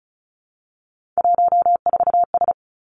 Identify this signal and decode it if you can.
Morse code, heard as a keyed tone, about 36 wpm, 700 Hz. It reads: 14S
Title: Three Challenging Letter-Number Combinations 35wpm